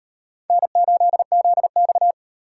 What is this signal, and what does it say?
Morse code, heard as a keyed tone, about 36 wpm, 700 Hz. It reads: N8ZX